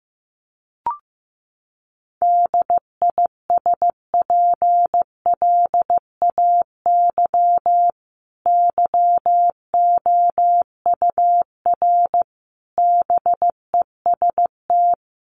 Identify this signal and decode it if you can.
Morse code, heard as a keyed tone, about 15 wpm, 700 Hz. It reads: DISPLAY YOUR BEST
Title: Sentences from Top 1200 Words 15wpm